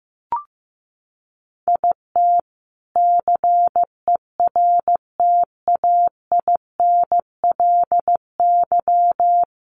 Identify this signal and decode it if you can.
Morse code, heard as a keyed tone, about 15 wpm, 700 Hz. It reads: IT CERTAINLY
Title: Sentences from Top 1000 Words 15wpm